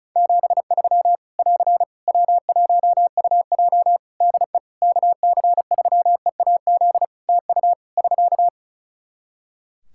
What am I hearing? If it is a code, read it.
Morse code, heard as a keyed tone, about 36 wpm, 700 Hz. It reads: 73 <AR> W1UJ DE KC3EAZ TU <SK>
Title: Practice QSO - Exchange 4 - Ending - 35wpm